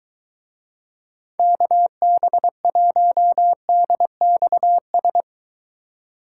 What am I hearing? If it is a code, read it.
Morse code, heard as a keyed tone, about 23 wpm, 700 Hz. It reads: KB1DXS